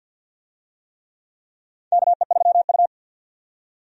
Morse code, heard as a keyed tone, about 49 wpm, 700 Hz. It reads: KE3U
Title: USA Call Signs - 4 Characters - 50wpm